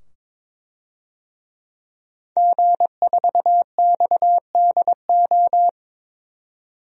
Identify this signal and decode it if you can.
Morse code, heard as a keyed tone, about 21 wpm, 700 Hz. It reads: G4XDO
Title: DXCC-100 Call Signs - 5 Characters - 22wpm